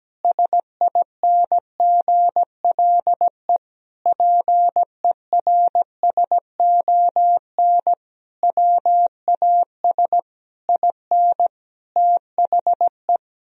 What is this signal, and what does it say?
Morse code, heard as a keyed tone, about 17 wpm, 700 Hz. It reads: SINGLE PERSON WAS IN THE
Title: Sentences from Top 1000 Words - Review Part 4 - 17wpm